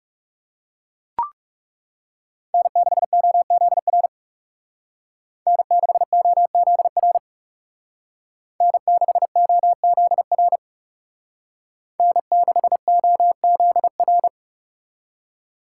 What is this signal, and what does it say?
Morse code, heard as a keyed tone, about 34 wpm, 700 Hz. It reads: N6OZR N6OZR N6OZR N6OZR
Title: Speed Racing - USA 1x3 Call Signs 30wpm